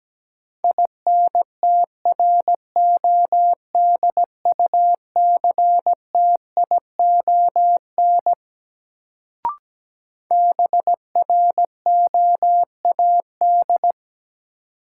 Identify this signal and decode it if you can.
Morse code, heard as a keyed tone, about 17 wpm, 700 Hz. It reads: INTRODUCTION BROAD
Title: Sentences from Top 1000 Words - New Word Drill - 17wpm